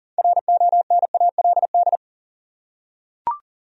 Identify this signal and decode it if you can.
Morse code, heard as a keyed tone, about 40 wpm, 700 Hz. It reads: RONALD
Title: USA Common Names - 40wpm